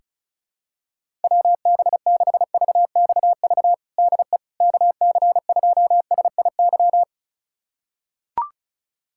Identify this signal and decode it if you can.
Morse code, heard as a keyed tone, about 36 wpm, 700 Hz. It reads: WB6VXV DE KC2SIY
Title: USA Callsign DE Callsign 35wpm